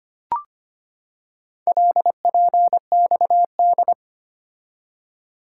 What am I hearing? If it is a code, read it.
Morse code, heard as a keyed tone, about 25 wpm, 700 Hz. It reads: LPXD
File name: practice / 25wpm / Four Letter-Number Combinations - 25wpm